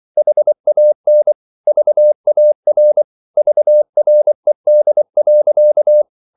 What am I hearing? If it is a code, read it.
Morse code, heard as a keyed tone, about 24 wpm, 600 Hz. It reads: HAN VAR VRED.
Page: Ferie 24 wpm